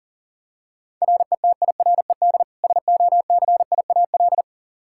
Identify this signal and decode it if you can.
Morse code, heard as a keyed tone, about 37 wpm, 700 Hz. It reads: RETIRED SOCIAL
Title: K1USN SST QSO Practice 40wpm